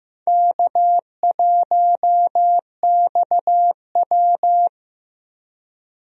Morse code, heard as a keyed tone, about 15 wpm, 700 Hz. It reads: K1XW
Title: USA Call Signs - 4 Characters - Repeated 2x - 15wpm